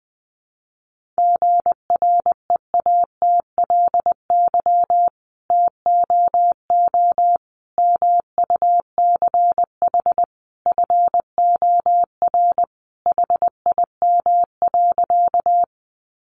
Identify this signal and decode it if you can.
Morse code, heard as a keyed tone, about 20 wpm, 700 Hz. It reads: GREATLY TOO MUCH FOR HIM.
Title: Sentences from Top 1400 Words 20wpm